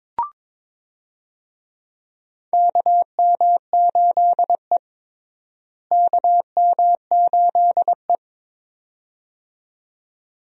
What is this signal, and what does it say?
Morse code, heard as a keyed tone, about 22 wpm, 700 Hz. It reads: KM8E KM8E
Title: USA Call Signs - 4 Characters - Repeated 2x - 22wpm